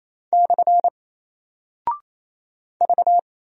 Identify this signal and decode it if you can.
Morse code, heard as a keyed tone, about 27 wpm, 700 Hz. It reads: / V
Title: Single Character (Letter:Number:Punctuation:Prosign) 28wpm